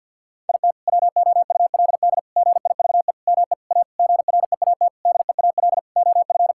Morse code, heard as a keyed tone, about 50 wpm, 700 Hz. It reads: IT WOULD GIVE ME A GREAT DEAL OF
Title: Sentences from Top 1600 Words 50wpm